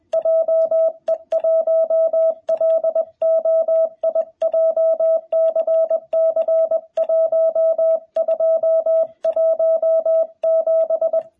Morse code, heard as a keyed tone, about 20 wpm, 660 Hz, with radio noise. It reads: JE1LOI JCC1217
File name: jh8cbh / 2021cw